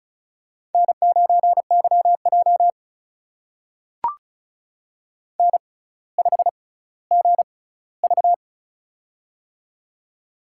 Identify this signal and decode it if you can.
Morse code, heard as a keyed tone, about 36 wpm, spaced as for 18 wpm, 700 Hz. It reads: N9YJ N5GV
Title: USA Call Signs - 4 Characters - 6x Character Spacing - 35wpm